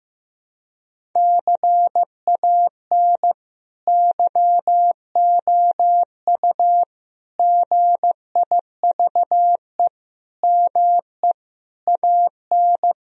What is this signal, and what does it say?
Morse code, heard as a keyed tone, about 15 wpm, 700 Hz. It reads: CAN YOU GIVE ME AN
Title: Sentences from Top 1000 Words - Review Part 1 - 15wpm